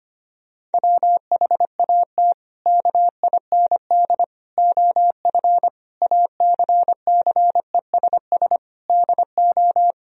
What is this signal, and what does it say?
Morse code, heard as a keyed tone, about 25 wpm, 700 Hz. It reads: WHAT KIND OF ACCESS DO
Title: Sentences from Top 1000 Words - Review Part 4 - 25wpm